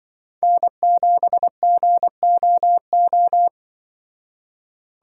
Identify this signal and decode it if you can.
Morse code, heard as a keyed tone, about 24 wpm, 700 Hz. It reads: N7GOO
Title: Speed Racing - USA 1x3 Call Signs 20wpm